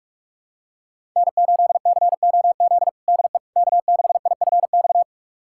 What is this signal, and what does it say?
Morse code, heard as a keyed tone, about 46 wpm, 700 Hz. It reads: N8COZ DE K6IFX
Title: USA Callsign DE Callsign - 5 Characters - 45wpm